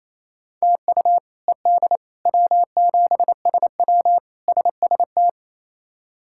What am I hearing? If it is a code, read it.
Morse code, heard as a keyed tone, about 27 wpm, 700 Hz. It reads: TU ED W7SW SST
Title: K1USN SST QSO Practice - 28wpm